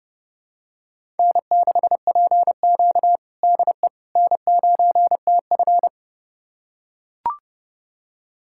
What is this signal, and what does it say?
Morse code, heard as a keyed tone, about 30 wpm, 700 Hz. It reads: N6PQ DE N9TF
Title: USA Callsign DE Callsign - 4 Characters - 30wpm